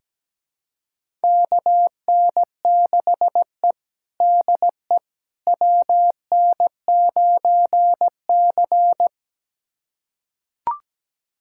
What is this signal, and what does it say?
Morse code, heard as a keyed tone, about 17 wpm, 700 Hz. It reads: KN6E DE WN9C
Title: USA Callsign DE Callsign - 4 Characters - 17wpm